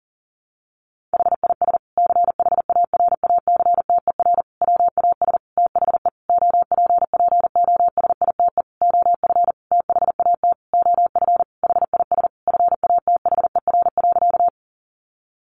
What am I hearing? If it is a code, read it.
Morse code, heard as a keyed tone, about 40 wpm, 700 Hz. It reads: HIS CHARACTER WAS THE OPPOSITE OF THAT OF HIS FATHER.
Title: Sentences from Top 1300 Words 40wpm